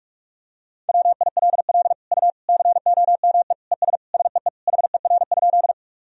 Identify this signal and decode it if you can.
Morse code, heard as a keyed tone, about 41 wpm, 700 Hz. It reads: WILL U COME ES SEE HER?
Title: Sentences from Top 100 Words and Abbreviations 45wpm